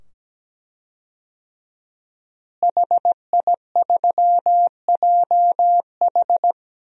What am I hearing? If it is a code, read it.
Morse code, heard as a keyed tone, about 17 wpm, 700 Hz. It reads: HI3JH
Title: DXCC-100 Call Signs - 5 Characters - Repeated 2x - 17wpm